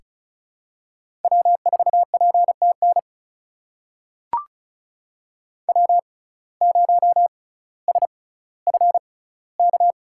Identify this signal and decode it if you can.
Morse code, heard as a keyed tone, about 36 wpm, spaced as for 18 wpm, 700 Hz. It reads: W4PTN W0SFK
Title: USA Call Signs - 5 Characters - 6x Character Spacing - 35wpm